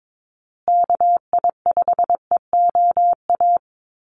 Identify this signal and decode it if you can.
Morse code, heard as a keyed tone, about 22 wpm, 700 Hz. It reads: KI5EOA
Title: USA Call Signs Repeated 3x 22wpm